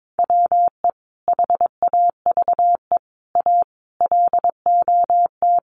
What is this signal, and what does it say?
Morse code, heard as a keyed tone, about 22 wpm, 700 Hz. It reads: WE HAVE A LOT
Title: Sets of 5 Words 22wpm